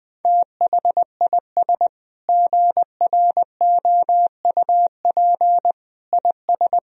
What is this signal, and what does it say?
Morse code, heard as a keyed tone, about 21 wpm, 700 Hz. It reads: THIS GROUP IS
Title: Sentences from Top 1000 Words - Review Part 1 - 20wpm